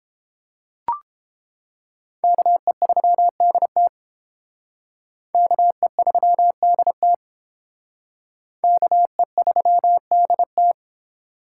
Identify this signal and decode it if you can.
Morse code, heard as a keyed tone, about 28 wpm, 700 Hz. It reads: KE3DT KE3DT KE3DT